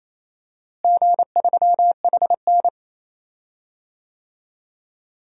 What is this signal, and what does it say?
Morse code, heard as a keyed tone, about 28 wpm, 700 Hz. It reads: G3HN
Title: DXCC-100 Call Signs - 4 Characters - Repeated 3x - 28wpm